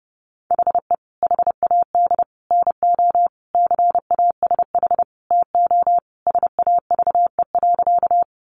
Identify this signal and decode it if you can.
Morse code, heard as a keyed tone, about 30 wpm, 700 Hz. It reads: HE HAD NO CASH TO SAVE.
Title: Sentences from Top 1100 Words 30wpm